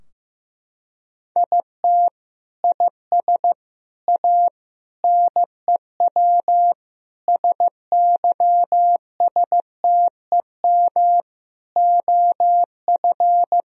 Morse code, heard as a keyed tone, about 15 wpm, 700 Hz. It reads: IT IS A NEW SYSTEM OF
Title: Sentences from Top 1000 Words - Review Part 3 - 15wpm